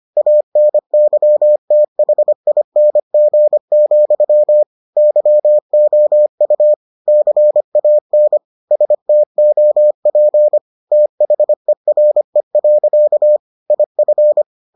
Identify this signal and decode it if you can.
Morse code, heard as a keyed tone, about 25 wpm, 600 Hz. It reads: ANYTHING, YOU CAN STOP THERE. IF